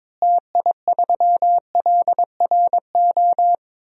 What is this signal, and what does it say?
Morse code, heard as a keyed tone, about 22 wpm, 700 Hz. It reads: TI3LRO
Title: DXCC-100 Call Signs - 6 Characters - Repeated 2x - 22wpm